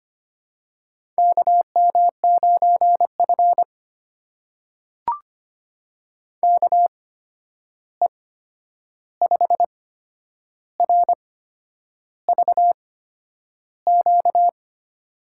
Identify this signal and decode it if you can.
Morse code, heard as a keyed tone, about 25 wpm, spaced as for 9 wpm, 700 Hz. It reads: KM9F KE5RVQ
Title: USA Call Signs - 8x Character Spacing - 25wpm